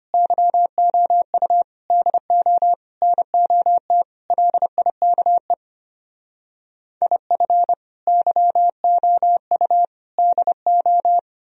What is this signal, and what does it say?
Morse code, heard as a keyed tone, about 32 wpm, 700 Hz. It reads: YOU DO NOT LIKE IF YOU DO